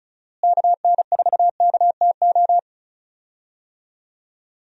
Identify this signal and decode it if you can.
Morse code, heard as a keyed tone, about 34 wpm, 700 Hz. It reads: KN4KTO
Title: USA Call Signs - 6 Characters - Repeated 3x - 35wpm